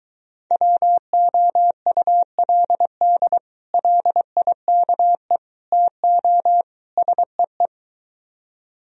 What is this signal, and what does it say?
Morse code, heard as a keyed tone, about 23 wpm, 700 Hz. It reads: WOULD LIKE TO SEE